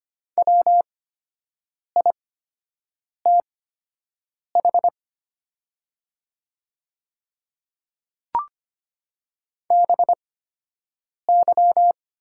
Morse code, heard as a keyed tone, about 25 wpm, spaced as for 7 wpm, 700 Hz. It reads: WITH BY
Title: 8x Character Spacing - Top 100 Words - 25wpm